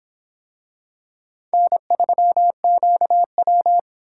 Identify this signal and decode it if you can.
Morse code, heard as a keyed tone, about 26 wpm, 700 Hz. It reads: N3QW